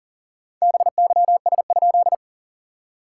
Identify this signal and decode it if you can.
Morse code, heard as a keyed tone, about 40 wpm, 690 Hz. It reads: BYS?
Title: SOTA QSO Practice 40wpm